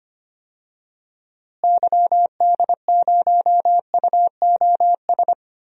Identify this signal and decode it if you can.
Morse code, heard as a keyed tone, about 25 wpm, 700 Hz. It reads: YD0UOS
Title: DXCC-100 Call Signs - 6 Characters - Repeated 2x - 25wpm